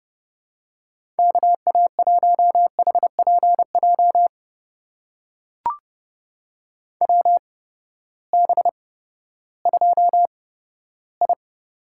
Morse code, heard as a keyed tone, about 30 wpm, spaced as for 15 wpm, 700 Hz. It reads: KA1HPJ WB2I
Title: USA Call Signs - 6 Characters - 8x Character Spacing - 30wpm